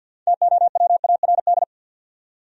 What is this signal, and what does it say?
Morse code, heard as a keyed tone, about 63 wpm, 700 Hz. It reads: TOWARD